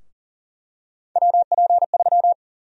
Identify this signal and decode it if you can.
Morse code, heard as a keyed tone, about 40 wpm, 700 Hz. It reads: WP3
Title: USA Call Sign Prefixes - Repeated 2x - 40wpm